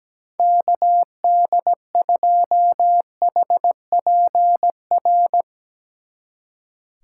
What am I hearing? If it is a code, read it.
Morse code, heard as a keyed tone, about 18 wpm, 700 Hz. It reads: KD2HPR
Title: Speed Racing - USA 2x3 Call Signs 17wpm